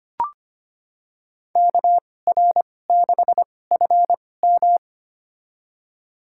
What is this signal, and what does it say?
Morse code, heard as a keyed tone, about 25 wpm, spaced as for 18 wpm, 700 Hz. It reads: KR6FM
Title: USA Call Signs - 5 Characters - 2x Character Spacing - 25wpm